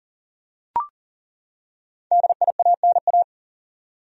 Morse code, heard as a keyed tone, about 39 wpm, 700 Hz. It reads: DIANA
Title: USA Common Names - Max 5 Letters - 40wpm